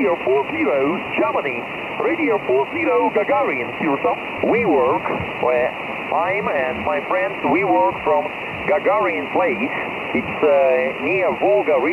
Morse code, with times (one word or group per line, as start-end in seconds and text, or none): none